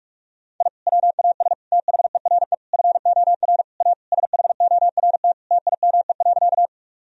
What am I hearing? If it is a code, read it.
Morse code, heard as a keyed tone, about 46 wpm, 700 Hz. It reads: I WAS THERE FOR A SHORT TIME.
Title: Sentences from Top 1000 Words - Review Part 2 - 45wpm